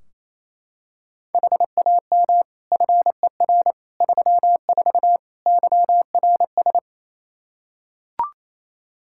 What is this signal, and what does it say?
Morse code, heard as a keyed tone, about 28 wpm, 700 Hz. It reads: HAM FER 34 YRS
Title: Practice QSO - Exchange 3 - More Info - 28wpm